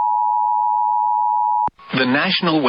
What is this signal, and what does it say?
Emergency Alert System (SAME) tones (the 853 and 960 Hz attention tone), with speech. It is heard from 0.0 to 1.7 s.